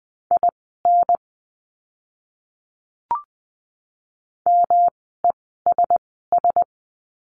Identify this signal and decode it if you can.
Morse code, heard as a keyed tone, about 20 wpm, spaced as for 14 wpm, 700 Hz. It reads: IN MESS
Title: 108 Lesson 18f - Letter m - Words 20wpm